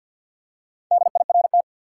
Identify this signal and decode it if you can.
Morse code, heard as a keyed tone, about 52 wpm, 700 Hz. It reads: DIRT
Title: Four Letter Words - 50wpm